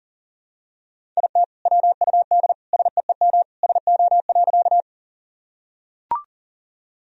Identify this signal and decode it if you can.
Morse code, heard as a keyed tone, about 40 wpm, 700 Hz. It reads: IT WUD SEEM SO.
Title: Sentences from Top 300 Words and Abbreviations 40wpm